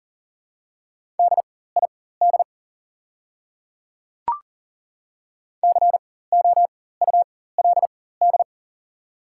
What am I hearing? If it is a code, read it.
Morse code, heard as a keyed tone, about 37 wpm, spaced as for 18 wpm, 700 Hz. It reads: DID COULD
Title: 4x Character Spacing - Top 200 Words - 40wpm